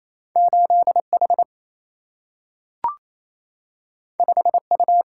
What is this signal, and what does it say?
Morse code, heard as a keyed tone, about 28 wpm, 700 Hz. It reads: 8 H 5 U
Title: Single-Letter Single-Number 28wpm